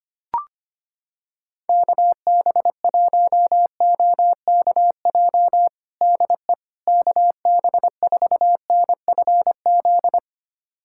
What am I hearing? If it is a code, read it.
Morse code, heard as a keyed tone, about 25 wpm, 700 Hz. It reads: KB1OKJ DE KB4NFZ
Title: USA Callsign DE Callsign 25wpm